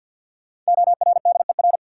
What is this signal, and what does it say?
Morse code, heard as a keyed tone, about 51 wpm, 700 Hz. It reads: ORDER
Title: Sets of 2 Words - New Word Drill - 50wpm